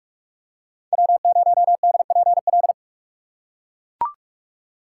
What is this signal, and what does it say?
Morse code, heard as a keyed tone, about 43 wpm, 700 Hz. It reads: W0DPL
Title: USA Call Signs - 5 Characters - 3x Character Spacing - 45wpm